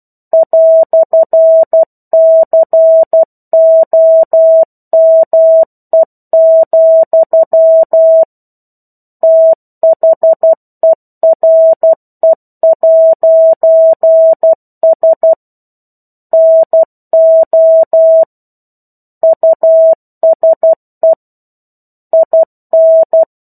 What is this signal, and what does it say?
Morse code, heard as a keyed tone, about 12 wpm, 650 Hz. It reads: "COME, THERE'S NO USE IN
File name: 01-Chapter-01 12 WPM